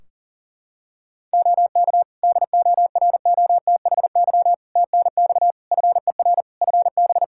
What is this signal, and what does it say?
Morse code, heard as a keyed tone, about 40 wpm, 700 Hz. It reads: OK DOROTHY TNX FER FB
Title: Practice QSO - Exchange 4 - Ending - 40wpm